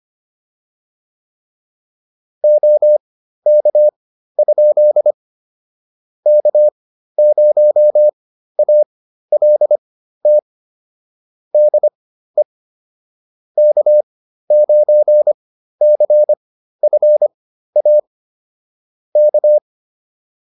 Morse code, heard as a keyed tone, about 25 wpm, spaced as for 13 wpm, 600 Hz. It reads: OK? K0ALT DE K9CFA K
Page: intermediate-practice-files